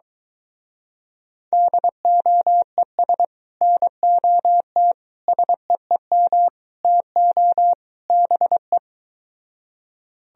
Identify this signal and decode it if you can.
Morse code, heard as a keyed tone, about 22 wpm, 700 Hz. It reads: DOES NOT SEEM TO BE